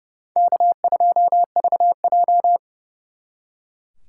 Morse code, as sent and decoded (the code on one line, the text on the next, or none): -.- ..--- ...- .---
K2VJ